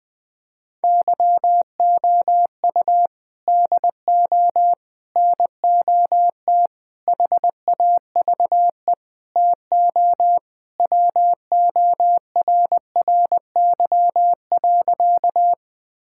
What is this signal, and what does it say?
Morse code, heard as a keyed tone, about 20 wpm, 700 Hz. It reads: YOU DO NOT HAVE TO WORRY.
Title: Sentences with 5-Word Phrases - 20wpm